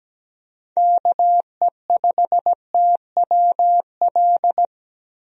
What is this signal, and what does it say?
Morse code, heard as a keyed tone, about 17 wpm, 700 Hz. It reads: KE5TWL
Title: USA Call Signs - 6 Characters - Repeated 3x - 17wpm